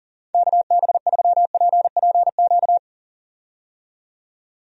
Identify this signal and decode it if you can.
Morse code, heard as a keyed tone, about 40 wpm, 700 Hz. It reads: KB3PPQ
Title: USA Call Signs - 6 Characters - 40wpm